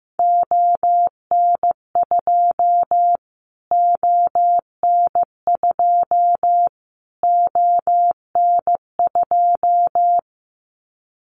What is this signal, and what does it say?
Morse code, heard as a keyed tone, about 15 wpm, 700 Hz. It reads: ON2 ON2 ON2
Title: Callsign Prefixes for Top 25 DXCC - Repeated 3x - 15wpm